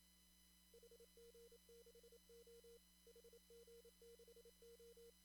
Morse code, heard as a keyed tone, about 28 wpm, 470 Hz, with noise. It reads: HG6O HG6O
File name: CW / kt5j wpx cw-op-k5pi